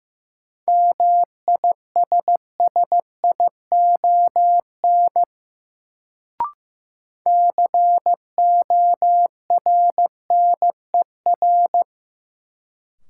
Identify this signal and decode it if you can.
Morse code, heard as a keyed tone, about 15 wpm, 700 Hz. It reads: MISSION CORNER
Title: Sentences from Top 1600 Words - New Word Drill Alt - 15wpm